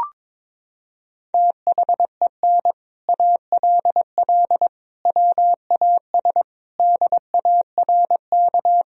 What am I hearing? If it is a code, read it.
Morse code, heard as a keyed tone, about 21 wpm, 700 Hz. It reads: THEN ALL WAS DARK
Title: Sentences from Top 500 Words and Abbreviations 22wpm